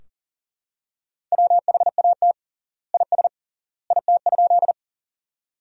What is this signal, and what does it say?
Morse code, heard as a keyed tone, about 40 wpm, 700 Hz. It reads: WHAT IS IT?
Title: 3x Word-Spacing - Sentences from Top 100 Words - 40wpm